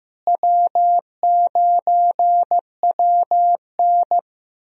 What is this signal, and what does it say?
Morse code, heard as a keyed tone, about 15 wpm, 700 Hz. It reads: W9WN
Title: USA Call Signs - 4 Characters - Repeated 3x - 15wpm